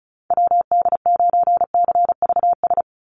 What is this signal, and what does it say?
Morse code, heard as a keyed tone, about 34 wpm, 700 Hz. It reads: WD9CVS
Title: USA Call Signs 35wpm